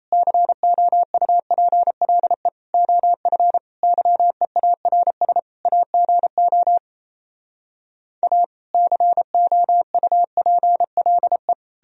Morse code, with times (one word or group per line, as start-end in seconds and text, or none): 0.1-2.5 COUPLE
2.7-3.6 OF
3.8-5.4 YEARS
5.7-6.8 AGO
8.2-8.4 A
8.7-11.5 COUPLE